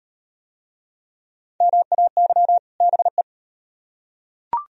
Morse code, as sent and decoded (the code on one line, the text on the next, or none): -- .- -.-- / -... .
MAY BE